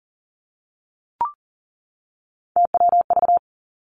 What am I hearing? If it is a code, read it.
Morse code, heard as a keyed tone, about 38 wpm, 700 Hz. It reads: TWV
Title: Three Challenging Letter-Number Combinations 40wpm